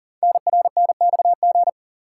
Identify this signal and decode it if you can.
Morse code, heard as a keyed tone, about 38 wpm, 700 Hz. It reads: NRNXG
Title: Code Groups - 5 Letters - 40wpm